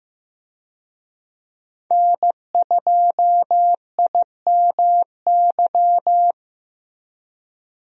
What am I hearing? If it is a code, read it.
Morse code, heard as a keyed tone, about 15 wpm, 700 Hz. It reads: N2IMY
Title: USA Call Signs - 5 Characters - Repeated 2x - 15wpm